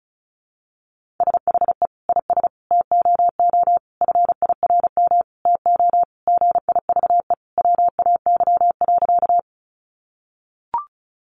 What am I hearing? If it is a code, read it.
Morse code, heard as a keyed tone, about 34 wpm, 700 Hz. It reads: SHE IS TOO FIRM TO GIVE WAY.
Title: Sentences from Top 1400 Words 35wpm